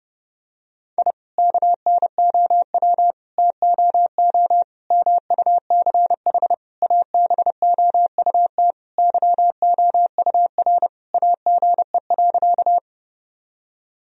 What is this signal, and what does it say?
Morse code, heard as a keyed tone, about 30 wpm, 700 Hz. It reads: I KNOW TOO MUCH ABOUT YOUR AGE.
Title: Sentences from Top 1000 Words - Review Part 3 - 30wpm